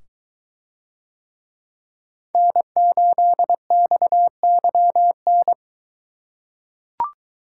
Morse code, heard as a keyed tone, about 22 wpm, 700 Hz. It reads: N8XYN